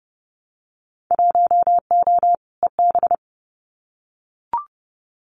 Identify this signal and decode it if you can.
Morse code, heard as a keyed tone, about 30 wpm, 700 Hz. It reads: 1O EB
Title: Winter Field Day Exchanges 30wpm